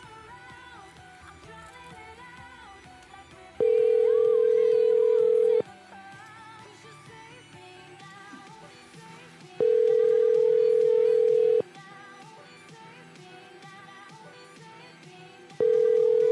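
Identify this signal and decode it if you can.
Ringback tone (North America), 440 + 480 Hz.
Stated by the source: Prank phone calls made on a show streamed live on YouTube.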